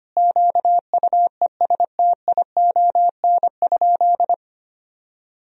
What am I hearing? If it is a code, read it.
Morse code, heard as a keyed tone, about 25 wpm, 700 Hz. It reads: QUESTION?
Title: Sentences from Top 1000 Words - Review Part 1 - 25wpm